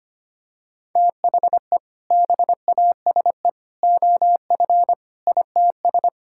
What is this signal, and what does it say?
Morse code, heard as a keyed tone, about 25 wpm, 700 Hz. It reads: THE BASE OF ITS
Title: Sentences from Top 1000 Words - Review Part 5 - 25wpm